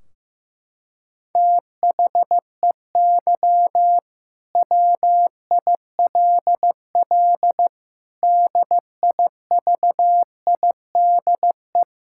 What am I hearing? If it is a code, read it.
Morse code, heard as a keyed tone, about 15 wpm, 700 Hz. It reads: THEY WILL DIVIDE
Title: Sentences from Top 1600 Words 15wpm